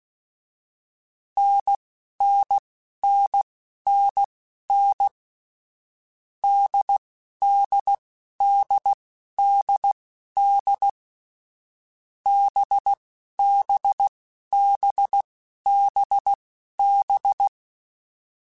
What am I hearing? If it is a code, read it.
Morse code, heard as a keyed tone, about 16 wpm, spaced as for 11 wpm, 800 Hz. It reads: NNNNN DDDDD BBBBB
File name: Short Runs / Short Runs-mp3s / NDB NDB6 @10